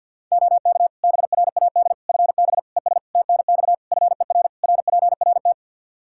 Morse code, heard as a keyed tone, about 50 wpm, 700 Hz. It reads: OK BRAD FB ES TNX FER RPRT
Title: K1USN SST QSO Practice 50wpm